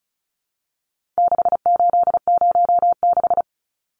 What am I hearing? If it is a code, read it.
Morse code, heard as a keyed tone, about 34 wpm, 700 Hz. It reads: 6806
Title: Numbers - 4-Digit 35wpm